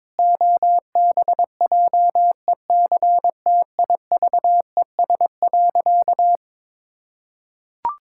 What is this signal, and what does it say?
Morse code, heard as a keyed tone, about 22 wpm, 700 Hz. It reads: OBJECTIVES.
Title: Pangram Challenge v2 - 22wpm